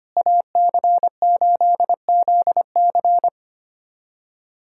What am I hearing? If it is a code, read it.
Morse code, heard as a keyed tone, about 25 wpm, 700 Hz. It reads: AC8ZC
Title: USA Callsign DE Callsign - 5 Characters - 25wpm